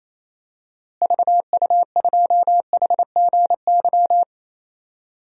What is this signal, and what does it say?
Morse code, heard as a keyed tone, about 27 wpm, 700 Hz. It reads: VU2HGY
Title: DXCC-100 Call Signs - 6 Characters - Repeated 3x - 28wpm